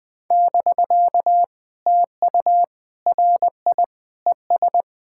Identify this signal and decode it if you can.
Morse code, heard as a keyed tone, about 20 wpm, 700 Hz. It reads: <BK> TU RI ES
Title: POTA QSO Practice - 20wpm